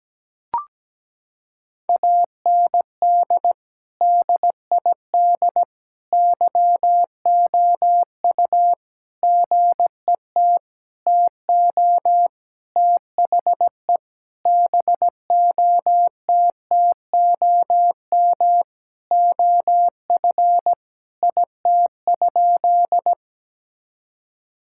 Morse code, heard as a keyed tone, about 17 wpm, 700 Hz. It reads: AND DID YOU GET TO THE BOTTOM OF IT?
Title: Sentences from Top 1000 Words - Review Part 6 - 17wpm